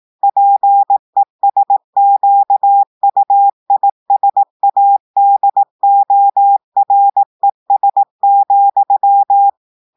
Morse code, heard as a keyed tone, about 18 wpm, 800 Hz. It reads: PESQUISADORES,
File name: audios / portugais / portugais texte en clair 1 vitesse 18 mots minutes